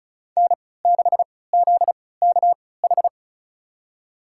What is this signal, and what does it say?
Morse code, heard as a keyed tone, about 34 wpm, spaced as for 20 wpm, 700 Hz. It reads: N6ZKH
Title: USA Call Signs - 5 Characters - 3x Character Spacing - 35wpm